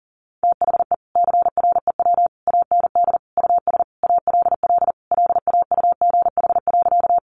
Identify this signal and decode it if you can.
Morse code, heard as a keyed tone, about 40 wpm, 700 Hz. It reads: THE CREW AND US ALL LAUGH.
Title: Sentences from Top 1500 Words 40wpm